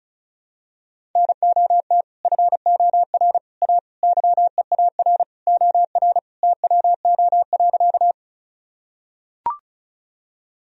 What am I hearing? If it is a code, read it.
Morse code, heard as a keyed tone, about 34 wpm, 700 Hz. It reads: NOT FOR A YEAR OR TWO.
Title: Sentences from Top 1000 Words - Review Part 1 - 35wpm